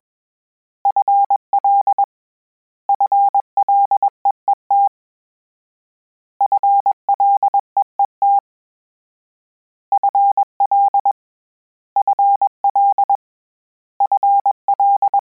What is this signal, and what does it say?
Morse code, heard as a keyed tone, about 21 wpm, 800 Hz. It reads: FL FLEET FLEET FL FL FL
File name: Combinations / Combinations-mp3s / Combinations@20